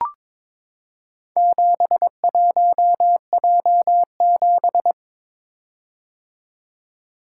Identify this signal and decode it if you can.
Morse code, heard as a keyed tone, about 22 wpm, 700 Hz. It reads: 71J7
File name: practice / 22wpm / Four Letter-Number Combinations - 22wpm